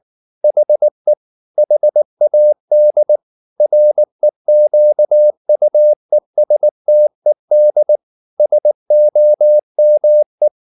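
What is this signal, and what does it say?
Morse code, heard as a keyed tone, about 19 wpm, 600 Hz. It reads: HE HAD REQUESTED SOME
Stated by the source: Morse code at 19 wpm from CW med Gnister, reading Ken Follett's novel Never in English.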